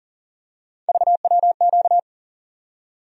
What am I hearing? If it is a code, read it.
Morse code, heard as a keyed tone, about 40 wpm, 700 Hz. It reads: VWQ
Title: USA Call Sign Postfixes - 3 Letters 40wpm